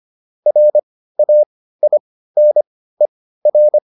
Morse code, heard as a keyed tone, about 25 wpm, spaced as for 15 wpm, 600 Hz. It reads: RAINER
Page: intermediate-practice-files